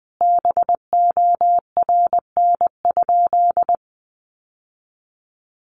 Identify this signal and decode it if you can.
Morse code, heard as a keyed tone, about 19 wpm, 700 Hz. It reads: BORN?
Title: Sentences from Top 1000 Words 20wpm